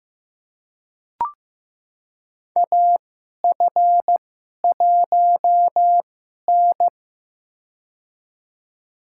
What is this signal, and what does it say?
Morse code, heard as a keyed tone, about 15 wpm, spaced as for 11 wpm, 700 Hz. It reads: AF1N
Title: USA Call Signs - 4 Characters - 2x Character Spacing - 15wpm